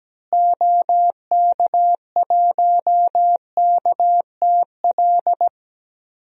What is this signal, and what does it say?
Morse code, heard as a keyed tone, about 17 wpm, 700 Hz. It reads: OK1KTL
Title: DXCC-100 Call Signs - 6 Characters - Repeated 2x - 17wpm